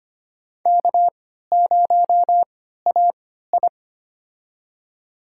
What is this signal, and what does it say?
Morse code, heard as a keyed tone, about 25 wpm, spaced as for 14 wpm, 700 Hz. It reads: K0AI
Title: USA Call Signs - 3x Character Spacing - 25wpm